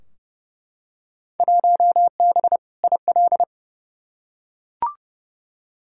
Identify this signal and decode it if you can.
Morse code, heard as a keyed tone, about 29 wpm, 700 Hz. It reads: 1B IL
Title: Field Day Exchanges Encore - 30wpm